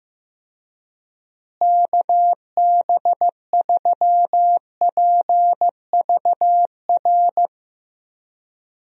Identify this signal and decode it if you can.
Morse code, heard as a keyed tone, about 15 wpm, 700 Hz. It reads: KB3PVR
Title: USA Call Signs - 6 Characters - Repeated 2x - 15wpm